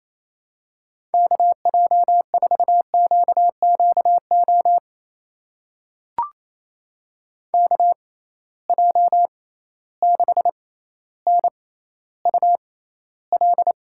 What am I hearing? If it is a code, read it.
Morse code, heard as a keyed tone, about 28 wpm, spaced as for 14 wpm, 700 Hz. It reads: KJ4QQO KJ6NUL
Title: USA Call Signs - 6 Characters - 6x Character Spacing - 28wpm